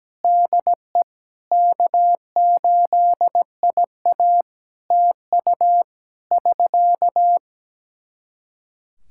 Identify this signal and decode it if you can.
Morse code, heard as a keyed tone, about 17 wpm, 700 Hz. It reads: DE K8IA TU <SK>
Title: K1USN SST QSO Practice 17wpm